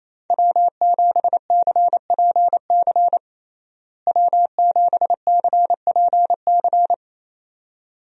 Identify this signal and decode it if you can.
Morse code, heard as a keyed tone, about 27 wpm, 700 Hz. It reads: W7CPC W7CPC
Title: USA Call Signs - 5 Characters - Repeated 2x - 28wpm